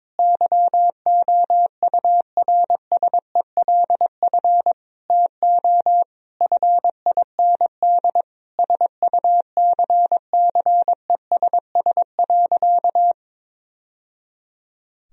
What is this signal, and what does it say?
Morse code, heard as a keyed tone, about 22 wpm, 700 Hz. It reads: YOURSELF TO FIND SUCCESS.
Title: Sentences from Top 1000 Words - Review Part 4 - 22wpm